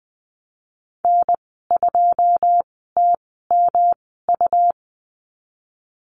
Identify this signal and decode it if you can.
Morse code, heard as a keyed tone, about 20 wpm, spaced as for 15 wpm, 700 Hz. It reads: N2TMU
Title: 147 Lesson 24g - Letter y - Call Signs 20wpm